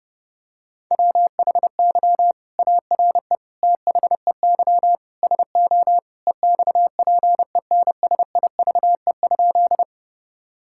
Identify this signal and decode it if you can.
Morse code, heard as a keyed tone, about 30 wpm, 700 Hz. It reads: WHY ARE THEY SO EXPENSIVE?
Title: Sentences from Top 1000 Words - Review Part 6 - 30wpm